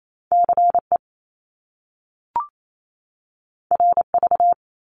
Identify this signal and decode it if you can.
Morse code, heard as a keyed tone, about 28 wpm, 700 Hz. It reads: CE RV
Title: Top Two Letter Combinations in Words 28wpm